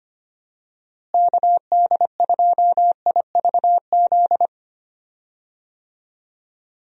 Morse code, heard as a keyed tone, about 25 wpm, 700 Hz. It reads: KD2IVZ
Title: USA Call Signs - 6 Characters - Repeated 2x - 25wpm